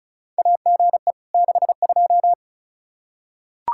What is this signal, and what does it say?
Morse code, heard as a keyed tone, about 34 wpm, 700 Hz. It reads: AGE 62
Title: Practice QSO - Exchange 3 - More Info - 35wpm